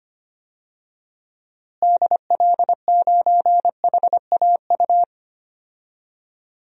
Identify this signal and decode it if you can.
Morse code, heard as a keyed tone, about 25 wpm, 700 Hz. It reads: DL9HAU
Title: DXCC-100 Call Signs - 6 Characters - Repeated 3x - 25wpm